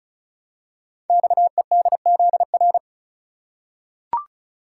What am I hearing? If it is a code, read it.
Morse code, heard as a keyed tone, about 36 wpm, 690 Hz. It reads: XEDZR
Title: Code Groups - 5 Letters - 35wpm